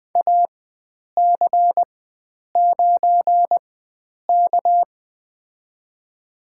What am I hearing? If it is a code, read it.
Morse code, heard as a keyed tone, about 20 wpm, spaced as for 9 wpm, 700 Hz. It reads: AC9K
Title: USA Call Signs - 4x Character Spacing - 20wpm